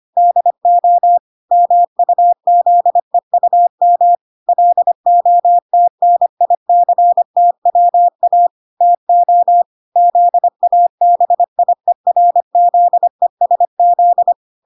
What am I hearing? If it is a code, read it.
Morse code, heard as a keyed tone, about 25 wpm, 700 Hz. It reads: DO MUZEUM LOTNICTWA TO ZABIERZESZ